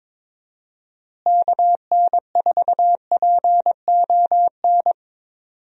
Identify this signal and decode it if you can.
Morse code, heard as a keyed tone, about 21 wpm, 700 Hz. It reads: KN4PON
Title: USA Call Signs - 6 Characters - Repeated 3x - 22wpm